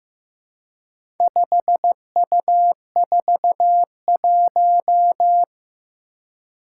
Morse code, heard as a keyed tone, about 15 wpm, 700 Hz. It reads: 5U41
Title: Four Challenging Letter-Number Combinations 15wpm